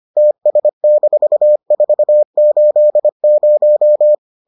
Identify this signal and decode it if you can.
Morse code, heard as a keyed tone, about 25 wpm, 600 Hz. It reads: TS-480